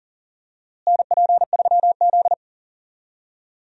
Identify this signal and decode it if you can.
Morse code, heard as a keyed tone, about 40 wpm, 700 Hz. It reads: NP3Z
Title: USA Call Signs - 4 Characters - Repeated 3x - 40wpm